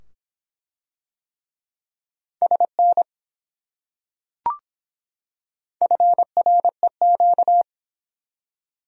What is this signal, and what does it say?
Morse code, heard as a keyed tone, about 24 wpm, 700 Hz. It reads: SN FREQ